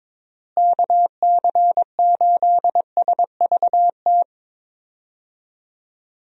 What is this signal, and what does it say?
Morse code, heard as a keyed tone, about 22 wpm, 700 Hz. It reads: KC8SVT
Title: USA Call Signs - 6 Characters - Repeated 3x - 22wpm